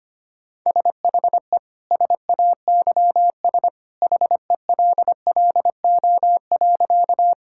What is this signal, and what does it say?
Morse code, heard as a keyed tone, about 25 wpm, 700 Hz. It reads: SHE SAYS HELLO.
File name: practice / 25wpm / Sentences with 2-Word Phrases - 25wpm